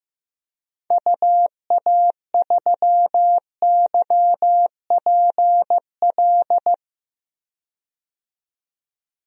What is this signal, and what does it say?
Morse code, heard as a keyed tone, about 15 wpm, 690 Hz. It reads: UA3YPL
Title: DXCC-100 Call Signs - 6 Characters - 15wpm